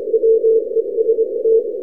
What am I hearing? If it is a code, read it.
Morse code, heard as a keyed tone, about 25 wpm, 450 Hz. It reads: WEST
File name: n6tr / oz1lxj-test